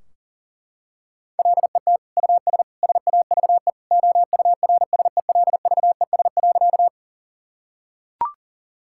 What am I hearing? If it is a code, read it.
Morse code, heard as a keyed tone, about 40 wpm, 700 Hz. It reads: LET US SAVE OURSELVES.
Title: Sentences from Top 1600 Words 40wpm